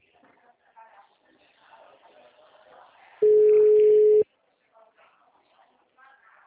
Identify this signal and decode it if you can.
Ringback tone (Europe, 425 Hz), with speech.